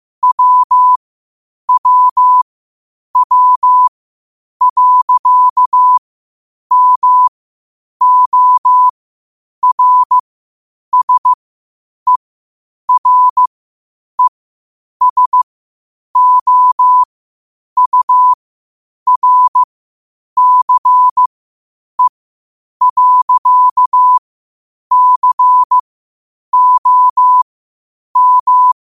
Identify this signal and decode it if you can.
Morse code, heard as a keyed tone, about 15 wpm, spaced as for 8 wpm, 1000 Hz. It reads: WWW.MORSERESOURCE.COM
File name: qotdfiles / 8wpm